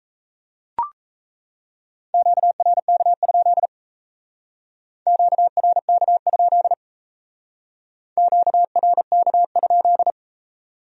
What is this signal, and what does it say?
Morse code, heard as a keyed tone, about 34 wpm, 700 Hz. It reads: QRK? QRK? QRK?
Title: Speed-Racing Q-Codes - 28wpm